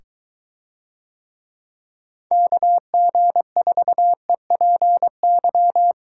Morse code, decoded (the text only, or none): KG4EPY